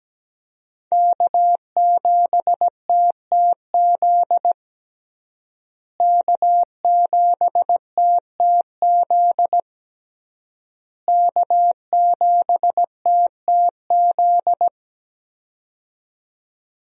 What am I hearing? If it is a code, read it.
Morse code, heard as a keyed tone, about 17 wpm, 700 Hz. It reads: K7TTZ K7TTZ K7TTZ
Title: USA Call Signs - 5 Characters - Repeated 3x - 17wpm